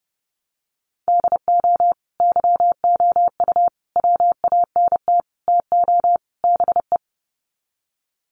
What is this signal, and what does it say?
Morse code, heard as a keyed tone, about 30 wpm, 700 Hz. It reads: DO YOU WANT TO BE
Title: Sets of 5 Words Encore2 30wpm